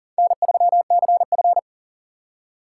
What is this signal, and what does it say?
Morse code, heard as a keyed tone, about 40 wpm, 700 Hz. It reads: N3CF
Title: USA Call Signs - 4 Characters - 4x Character Spacing - 40wpm